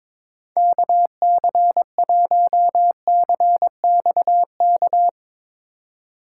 Morse code, heard as a keyed tone, about 22 wpm, 700 Hz. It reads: KC1CXK
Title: USA Call Signs - 6 Characters - 3x Character Spacing - 22wpm